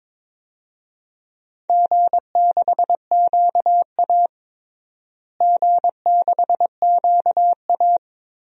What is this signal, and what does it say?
Morse code, heard as a keyed tone, about 22 wpm, 700 Hz. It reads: G6QA G6QA
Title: DXCC-100 Call Signs - 4 Characters - Repeated 3x - 22wpm